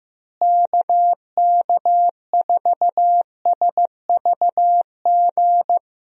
Morse code, heard as a keyed tone, about 15 wpm, 700 Hz. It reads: KK4SVG
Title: USA Call Signs - 6 Characters - Repeated 3x - 15wpm